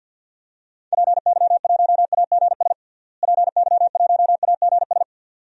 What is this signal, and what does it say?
Morse code, heard as a keyed tone, about 46 wpm, 700 Hz. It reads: PY1AGS PY1AGS
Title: DXCC-100 Call Signs - 6 Characters - Repeated 2x - 50wpm